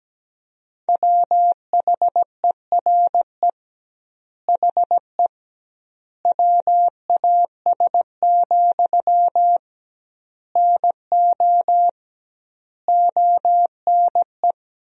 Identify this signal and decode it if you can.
Morse code, heard as a keyed tone, about 17 wpm, 700 Hz. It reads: WHERE HE WAS, NO ONE
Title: 2x Word-Spacing - Sentences from Top 400 Words - 17wpm